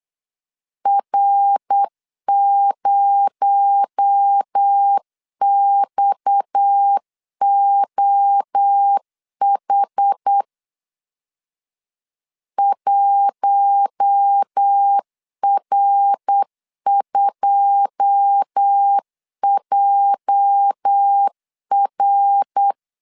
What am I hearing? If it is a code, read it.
Morse code, heard as a keyed tone, about 8 wpm, 800 Hz. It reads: R0XOH 1R2JR
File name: CW / DET / 1 9 1500C S C 8wpm 5653